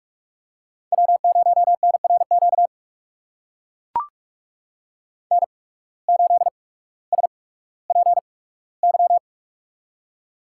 Morse code, heard as a keyed tone, about 44 wpm, spaced as for 19 wpm, 700 Hz. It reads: W0NRQ N8SPY
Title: USA Call Signs - 5 Characters - 8x Character Spacing - 45wpm